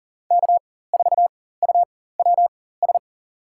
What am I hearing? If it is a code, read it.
Morse code, heard as a keyed tone, about 40 wpm, spaced as for 19 wpm, 700 Hz. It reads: K4UWS
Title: USA Call Signs - 5 Characters - 4x Character Spacing - 40wpm